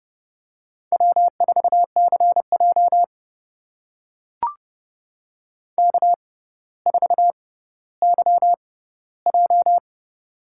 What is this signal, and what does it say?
Morse code, heard as a keyed tone, about 30 wpm, spaced as for 15 wpm, 700 Hz. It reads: W4CJ K4YJ
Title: USA Call Signs - 4 Characters - 6x Character Spacing - 30wpm